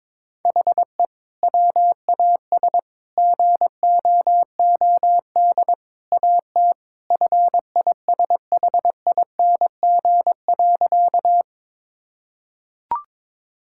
Morse code, heard as a keyed tone, about 22 wpm, 700 Hz. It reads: HE WAS GOOD AT FISHING.
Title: Sentences from Top 1000 Words - Review Part 5 - 22wpm